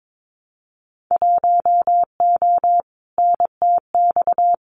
Morse code, heard as a keyed tone, about 22 wpm, 700 Hz. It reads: 1O NTX
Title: Winter Field Day Exchanges 22wpm